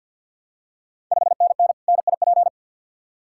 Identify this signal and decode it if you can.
Morse code, heard as a keyed tone, about 48 wpm, 700 Hz. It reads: 5NN NIP
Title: Ontario QSO Party Exchange 50wpm